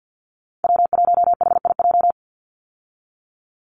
Morse code, heard as a keyed tone, about 48 wpm, 700 Hz. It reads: W1HIJ
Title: Call Signs Repeated 2x - 50wpm